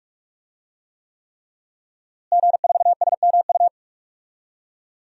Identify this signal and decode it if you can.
Morse code, heard as a keyed tone, about 46 wpm, 700 Hz. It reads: G4SMU
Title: DXCC-100 Call Signs - 5 Characters - Repeated 3x - 45wpm